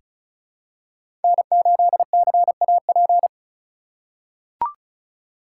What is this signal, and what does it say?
Morse code, heard as a keyed tone, about 36 wpm, 700 Hz. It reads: N8CAP
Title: USA Call Signs - 5 Characters - 3x Character Spacing - 35wpm